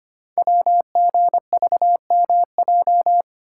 Morse code, heard as a keyed tone, about 25 wpm, 690 Hz. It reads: WGVMJ
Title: Code Groups - 5 Letters - 25wpm